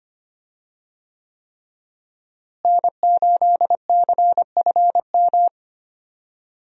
Morse code, heard as a keyed tone, about 25 wpm, 700 Hz. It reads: N8CFM
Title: USA Call Signs - 5 Characters - Repeated 3x - 25wpm